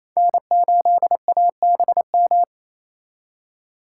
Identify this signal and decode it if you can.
Morse code, heard as a keyed tone, about 28 wpm, 700 Hz. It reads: N8ABM
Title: USA Call Signs - 5 Characters - 2x Character Spacing - 28wpm